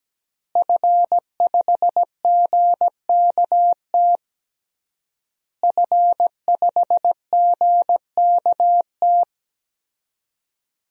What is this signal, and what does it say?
Morse code, heard as a keyed tone, about 17 wpm, 700 Hz. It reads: F5GKT F5GKT
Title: DXCC-100 Call Signs - 5 Characters - Repeated 3x - 17wpm